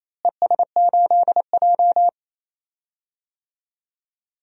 Morse code, heard as a keyed tone, about 28 wpm, 700 Hz. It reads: ES8J
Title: DXCC-100 Call Signs - 4 Characters - Repeated 2x - 28wpm